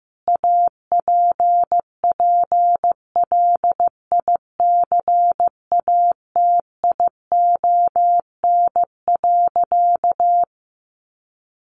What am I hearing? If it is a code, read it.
Morse code, heard as a keyed tone, about 15 wpm, 700 Hz. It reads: APPLICATION.
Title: Sentences from Top 1200 Words 15wpm